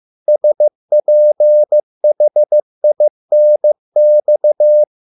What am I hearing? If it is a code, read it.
Morse code, heard as a keyed tone, about 15 wpm, 600 Hz. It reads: SPHINX